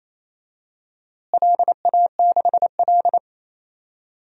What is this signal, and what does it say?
Morse code, heard as a keyed tone, about 28 wpm, 700 Hz. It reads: LA6L
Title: DXCC-100 Call Signs - 4 Characters - Repeated 3x - 28wpm